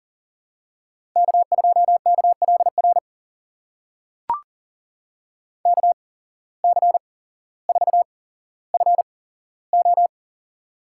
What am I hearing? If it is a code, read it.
Morse code, heard as a keyed tone, about 40 wpm, spaced as for 17 wpm, 700 Hz. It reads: K2KLR KC4FO
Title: USA Call Signs - 5 Characters - 8x Character Spacing - 40wpm